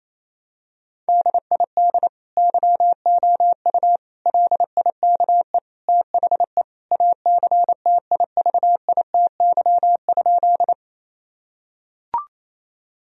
Morse code, heard as a keyed tone, about 28 wpm, 700 Hz. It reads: DID YOU LIKE THE ACTIVITY?
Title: Sentences from Top 1000 Words - Review Part 4 - 28wpm